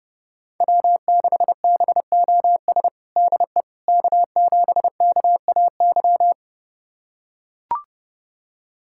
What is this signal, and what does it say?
Morse code, heard as a keyed tone, about 30 wpm, 700 Hz. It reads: W6BOS DE K7KAY
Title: USA Callsign DE Callsign - 5 Characters - 30wpm